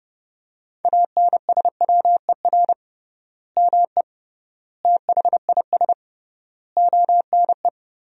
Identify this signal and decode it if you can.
Morse code, heard as a keyed tone, about 30 wpm, 700 Hz. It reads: ANSWER ME THIS ONE
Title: 3x Word-Spacing - Sentences from Top 400 Words - 30wpm